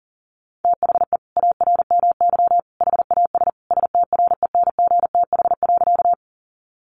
Morse code, heard as a keyed tone, about 38 wpm, 700 Hz. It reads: THE ARMY HAS STRENGTH.
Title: Sentences from Top 1300 Words 40wpm